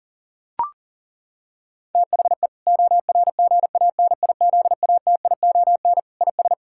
Morse code, heard as a keyed tone, about 40 wpm, 700 Hz. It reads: THE ORGANIZATION IS
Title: Sentences from Top 1000 Words - Review Part 5 - 40wpm